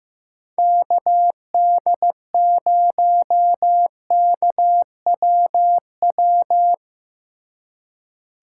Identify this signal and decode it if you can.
Morse code, heard as a keyed tone, about 17 wpm, 700 Hz. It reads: KD0KWW
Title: Speed Racing - USA 2x3 Call Signs 15wpm